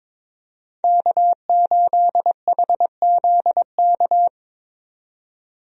Morse code, heard as a keyed tone, about 22 wpm, 700 Hz. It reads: K8HZK
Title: USA Call Signs - 5 Characters - 22wpm